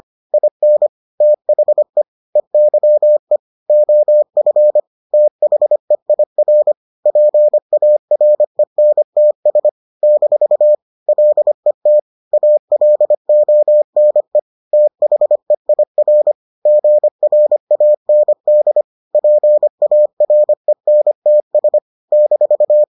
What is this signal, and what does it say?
Morse code, heard as a keyed tone, about 25 wpm, 600 Hz. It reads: IN THE EYE OF THEIR PARENTS - LET ALONE THEIR GRAND PARENTS -